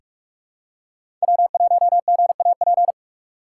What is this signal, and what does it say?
Morse code, heard as a keyed tone, about 43 wpm, 700 Hz. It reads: W1GAP
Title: USA Call Signs - 5 Characters - 3x Character Spacing - 45wpm